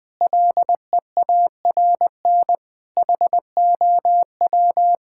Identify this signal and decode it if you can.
Morse code, heard as a keyed tone, about 20 wpm, 700 Hz. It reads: LEARN HOW
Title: Sets of 6 Words - 20wpm